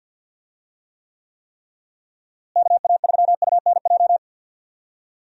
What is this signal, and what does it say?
Morse code, heard as a keyed tone, about 47 wpm, 700 Hz. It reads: KA3UNJ
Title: USA Call Signs - 6 Characters - Repeated 3x - 50wpm